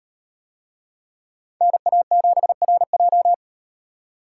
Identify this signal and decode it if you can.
Morse code, heard as a keyed tone, about 38 wpm, 700 Hz. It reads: NA7RJ